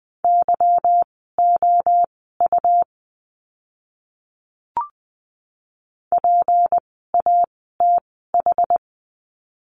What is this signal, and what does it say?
Morse code, heard as a keyed tone, about 20 wpm, spaced as for 15 wpm, 700 Hz. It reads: YOU PATH
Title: 153 Lesson 25f - Letter p - Words 20wpm